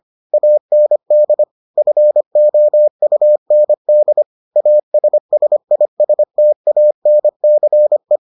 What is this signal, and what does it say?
Morse code, heard as a keyed tone, about 25 wpm, 600 Hz. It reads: AND FOUND ASSISTANCE